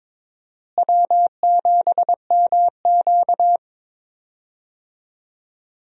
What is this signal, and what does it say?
Morse code, heard as a keyed tone, about 22 wpm, 700 Hz. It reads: W7MQ
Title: USA Call Signs - 4 Characters - Repeated 3x - 22wpm